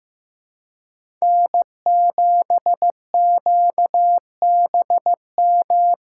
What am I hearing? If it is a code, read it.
Morse code, heard as a keyed tone, about 15 wpm, 700 Hz. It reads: N7QBM
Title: USA Call Signs - 5 Characters - 2x Character Spacing - 15wpm